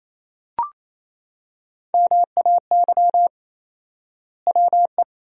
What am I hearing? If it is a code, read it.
Morse code, heard as a keyed tone, about 28 wpm, 700 Hz. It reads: MAY WE
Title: 4x Word-Spacing - Sentences from Top 100 Words - 28wpm